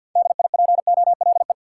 Morse code, heard as a keyed tone, about 52 wpm, 700 Hz. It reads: DIPOLE
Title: Top 90 QSO Elements 50wpm